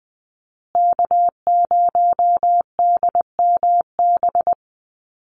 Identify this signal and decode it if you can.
Morse code, heard as a keyed tone, about 20 wpm, 700 Hz. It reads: K0DMB
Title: USA Call Signs Repeated 2x 20wpm